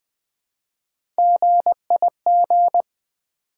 Morse code, heard as a keyed tone, about 20 wpm, 700 Hz. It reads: GIG
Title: Build A Word - 1 Letter At A Time - 3-Letter Words - 20wpm